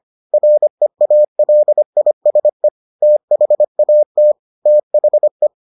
Morse code, heard as a keyed tone, about 25 wpm, 600 Hz. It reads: REALISE THAT THE